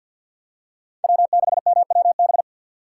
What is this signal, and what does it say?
Morse code, heard as a keyed tone, about 54 wpm, 700 Hz. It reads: W6MWB
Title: USA Call Signs - 5 Characters - 8x Character Spacing - 50wpm